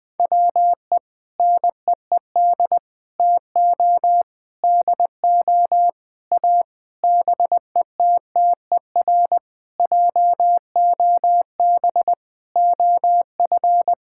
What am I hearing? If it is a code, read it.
Morse code, heard as a keyed tone, about 20 wpm, 700 Hz. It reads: WE NEED TO DO A BETTER JOB OF
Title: Sentences with 7-Word Phrases - 20wpm